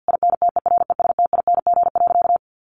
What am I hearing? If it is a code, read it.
Morse code, heard as a keyed tone, about 50 wpm, 700 Hz. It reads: INTERESTING.
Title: Sentences from Top 1200 Words 50wpm